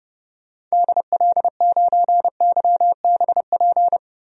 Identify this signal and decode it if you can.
Morse code, heard as a keyed tone, about 30 wpm, 700 Hz. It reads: DL9YBP
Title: DXCC-100 Call Signs - 6 Characters - Repeated 2x - 30wpm